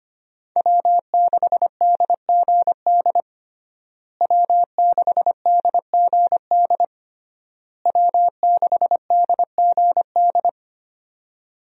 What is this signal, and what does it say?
Morse code, heard as a keyed tone, about 25 wpm, 700 Hz. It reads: W6DGD W6DGD W6DGD
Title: USA Call Signs - 5 Characters - Repeated 3x - 25wpm